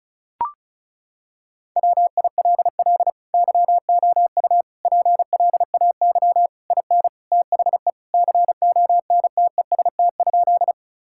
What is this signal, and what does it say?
Morse code, heard as a keyed tone, about 36 wpm, 700 Hz. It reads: WILL YOU PLAY IN THE CONTEST?
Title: Sentences from Top 1600 Words 35wpm